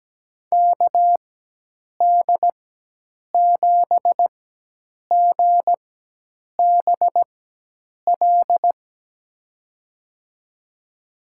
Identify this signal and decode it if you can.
Morse code, heard as a keyed tone, about 17 wpm, spaced as for 8 wpm, 700 Hz. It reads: KD7GBL
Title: USA Call Signs - 6 Characters - 4x Character Spacing - 17wpm